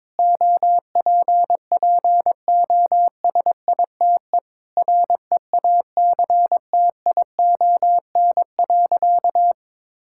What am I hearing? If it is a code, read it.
Morse code, heard as a keyed tone, about 22 wpm, 700 Hz. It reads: OPPOSITE REACTION.
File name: practice / 22wpm / Sentences with 8-Word Phrases - 22wpm